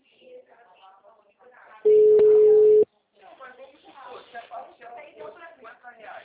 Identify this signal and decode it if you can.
Ringback tone (Europe, 425 Hz), with speech.
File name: out- -2015- - - .40735